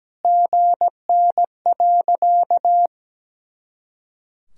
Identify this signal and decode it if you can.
Morse code, heard as a keyed tone, about 17 wpm, 700 Hz. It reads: GN.
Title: Sentences from Top 300 Words and Abbreviations 17wpm